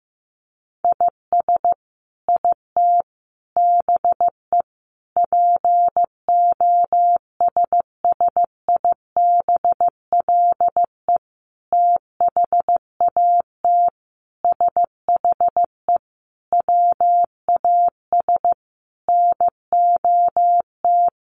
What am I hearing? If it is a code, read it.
Morse code, heard as a keyed tone, about 14 wpm, 700 Hz. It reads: IS IT BE POSSIBLE THAT SHE WAS NOT
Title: Sentences from Top 1200 Words 15wpm